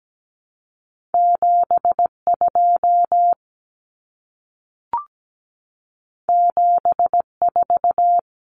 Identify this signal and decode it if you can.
Morse code, heard as a keyed tone, about 17 wpm, 700 Hz. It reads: 72 74
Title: Common Cut Numbers - 2-Digit 17wpm